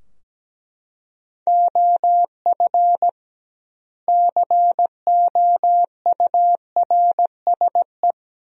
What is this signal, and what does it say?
Morse code, heard as a keyed tone, about 17 wpm, 700 Hz. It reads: OF COURSE
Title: 2x Word-Spacing - Sentences from Top 500 Words - 17wpm